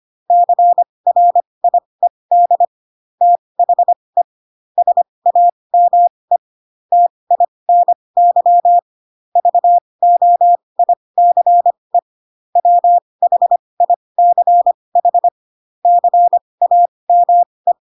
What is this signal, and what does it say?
Morse code, heard as a keyed tone, about 25 wpm, spaced as for 20 wpm, 700 Hz. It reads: CRIED THE SAME TINY VOICE WHICH CAME
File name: Pinocchio / Pinocchio ch2 15 25 ik1hge